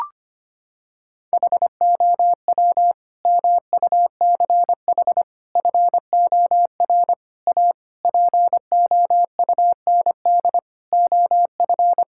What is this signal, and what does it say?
Morse code, heard as a keyed tone, about 25 wpm, 700 Hz. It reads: HOW MUCH FOR A POUND OF
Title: Sentences from Top 1000 Words - Review Part 5 - 25wpm